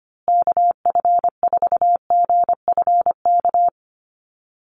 Morse code, heard as a keyed tone, about 25 wpm, 700 Hz. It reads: KF4GFK
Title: USA Call Signs Repeated 2x 25wpm